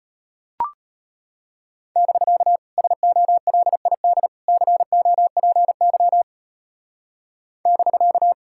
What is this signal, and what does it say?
Morse code, heard as a keyed tone, about 30 wpm, 700 Hz. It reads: <BK> SOLID COPY <BK>
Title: Speed-Racing Common QSO Phrases - 25wpm